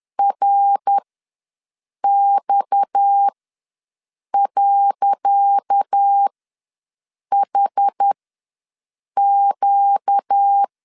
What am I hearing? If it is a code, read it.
Morse code, heard as a keyed tone, about 10 wpm, spaced as for 6 wpm, 800 Hz. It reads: RX.HQ
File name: LEZIONI / 10 SIGNS / 10C DOT ws 10wpm 4213